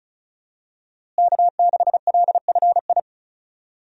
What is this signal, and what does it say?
Morse code, heard as a keyed tone, about 36 wpm, 700 Hz. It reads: K6LFI
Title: USA Call Signs - 5 Characters - Repeated 3x - 35wpm